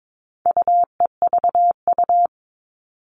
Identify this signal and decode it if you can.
Morse code, heard as a keyed tone, about 22 wpm, 700 Hz. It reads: UEVU
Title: All Cut Numbers - 4-Digit 22wpm